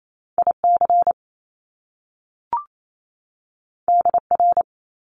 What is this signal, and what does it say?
Morse code, heard as a keyed tone, about 28 wpm, 700 Hz. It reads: IC DR
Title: Top Two Letter Combinations in Words 28wpm